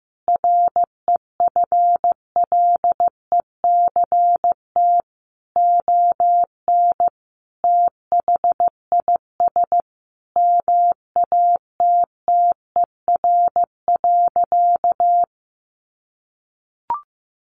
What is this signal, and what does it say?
Morse code, heard as a keyed tone, about 15 wpm, 700 Hz. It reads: REFLECT ON THIS MATTER.
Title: Sentences from Top 1500 Words 15wpm